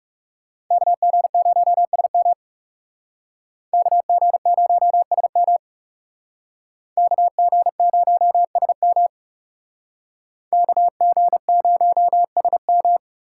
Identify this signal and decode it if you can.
Morse code, heard as a keyed tone, about 36 wpm, 700 Hz. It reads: KG0SM KG0SM KG0SM KG0SM
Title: Speed Racing - USA Call Signs 30wpm